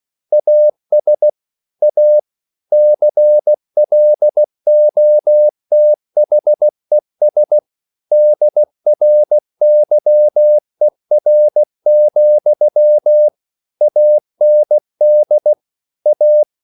Morse code, heard as a keyed tone, about 16 wpm, 600 Hz. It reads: AS A CLOTHES DRYER, AND A